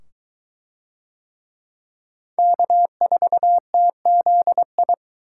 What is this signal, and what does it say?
Morse code, heard as a keyed tone, about 22 wpm, 700 Hz. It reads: K4TZI